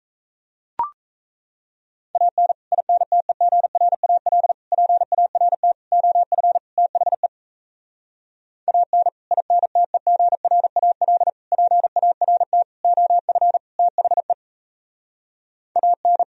Morse code, heard as a keyed tone, about 38 wpm, 700 Hz. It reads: AN INTEGRAL PART OF THE AN INTEGRAL PART OF THE AN